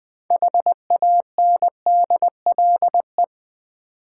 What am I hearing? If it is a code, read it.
Morse code, heard as a keyed tone, about 20 wpm, 700 Hz. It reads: HANDLE
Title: Sentences from Top 1100 Words - New Word Drill v2 - 20wpm